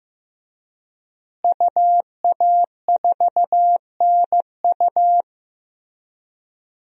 Morse code, heard as a keyed tone, about 15 wpm, 700 Hz. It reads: UA4NU
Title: DXCC-100 Call Signs - 5 Characters - Repeated 3x - 15wpm